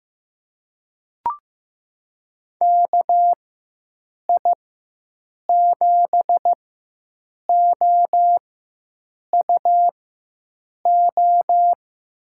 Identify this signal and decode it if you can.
Morse code, heard as a keyed tone, about 16 wpm, spaced as for 7 wpm, 700 Hz. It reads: KI7OUO
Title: USA Call Signs - 4x Character Spacing - 15wpm